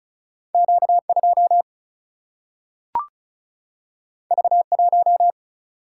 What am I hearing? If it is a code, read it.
Morse code, heard as a keyed tone, about 36 wpm, 700 Hz. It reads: Q 2 V 1
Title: Single-Letter Single-Number 35wpm